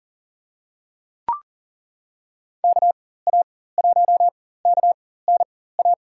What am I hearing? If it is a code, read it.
Morse code, heard as a keyed tone, about 40 wpm, spaced as for 19 wpm, 700 Hz. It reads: KA1KNA
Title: USA Call Signs - 6 Characters - 4x Character Spacing - 40wpm